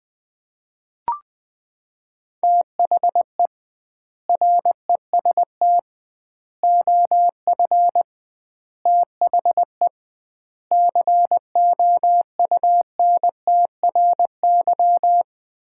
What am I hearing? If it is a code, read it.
Morse code, heard as a keyed tone, about 20 wpm, 700 Hz. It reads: THE REST OF THE COUNTRY
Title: 2x Word-Spacing - Sets of 5 Words - 20wpm